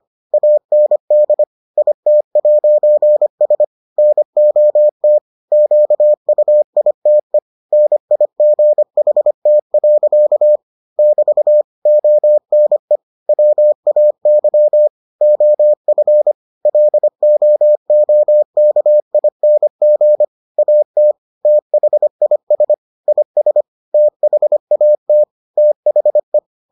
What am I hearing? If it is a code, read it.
Morse code, heard as a keyed tone, about 25 wpm, 600 Hz. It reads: AND IT'S NOT QUITE NIGHT. <BT> ONE WAY OF LOOKING AT THIS IS THAT THE